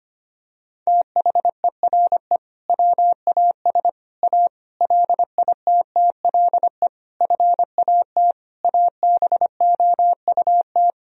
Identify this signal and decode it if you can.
Morse code, heard as a keyed tone, about 25 wpm, 700 Hz. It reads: THERE WAS A LITTLE FAT ABOUT
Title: Sentences from Top 1000 Words - Review Part 4 - 25wpm